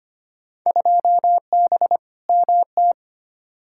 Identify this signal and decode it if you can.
Morse code, heard as a keyed tone, about 25 wpm, 700 Hz. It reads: 2B MT
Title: Field Day Exchanges Encore - 25wpm